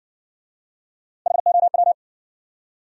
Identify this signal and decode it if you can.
Morse code, heard as a keyed tone, about 60 wpm, 700 Hz. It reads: HOW